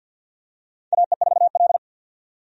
Morse code, heard as a keyed tone, about 51 wpm, 700 Hz. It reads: AE4L
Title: USA Call Signs - 4 Characters - 50wpm